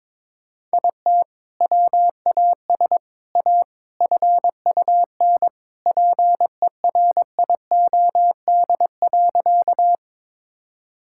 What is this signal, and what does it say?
Morse code, heard as a keyed tone, about 22 wpm, 700 Hz. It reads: IT WAS A FUN PERIOD.
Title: Sentences from Top 1000 Words - Review Part 5 - 22wpm